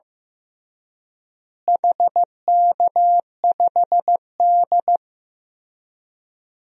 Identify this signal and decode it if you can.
Morse code, heard as a keyed tone, about 15 wpm, 700 Hz. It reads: HK5D
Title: DXCC-100 Call Signs - 4 Characters - Repeated 2x - 15wpm